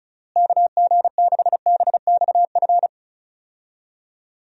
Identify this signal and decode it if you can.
Morse code, heard as a keyed tone, about 34 wpm, 700 Hz. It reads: KG6BXF
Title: USA Call Signs - 6 Characters - Repeated 3x - 35wpm